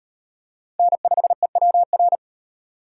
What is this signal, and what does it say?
Morse code, heard as a keyed tone, about 38 wpm, 700 Hz. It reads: N5EWR